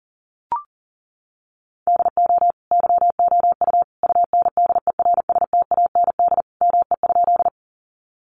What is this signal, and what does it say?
Morse code, heard as a keyed tone, about 40 wpm, 700 Hz. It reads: DO YOU UNDERSTAND ME?
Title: Sentences from Top 600 Words 40wpm